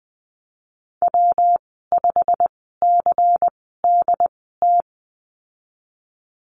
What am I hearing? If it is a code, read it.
Morse code, heard as a keyed tone, about 20 wpm, spaced as for 14 wpm, 700 Hz. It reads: W5CDT
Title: 147 Lesson 24g - Letter y - Call Signs 20wpm